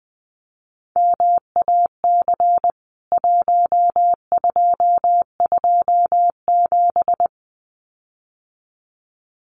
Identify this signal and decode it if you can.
Morse code, heard as a keyed tone, about 20 wpm, 700 Hz. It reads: MAC 1227
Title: CWT Exchange 20wpm